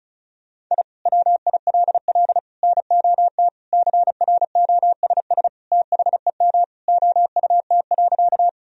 Morse code, heard as a keyed tone, about 34 wpm, 700 Hz. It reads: I WILL NOT CROSS THEM OUT.
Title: Sentences from Top 1000 Words - Review Part 1 - 35wpm